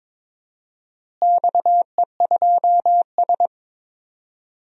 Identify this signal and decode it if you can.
Morse code, heard as a keyed tone, about 22 wpm, 700 Hz. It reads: XE2S
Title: DXCC-100 Call Signs - 4 Characters - Repeated 3x - 22wpm